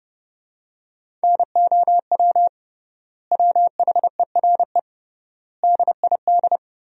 Morse code, heard as a keyed tone, about 30 wpm, 700 Hz. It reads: NOW WHERE DID
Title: 3x Word-Spacing - Sentences from Top 200 Words - 30wpm